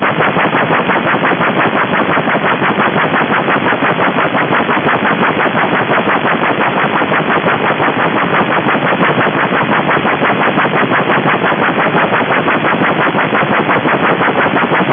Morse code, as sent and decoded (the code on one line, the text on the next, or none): none